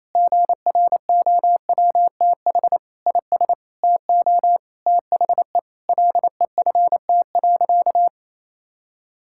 Morse code, heard as a keyed tone, about 28 wpm, 700 Hz. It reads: GROWTH IS TO THE LEFT.
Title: Sentences from Top 1000 Words - Review Part 5 - 28wpm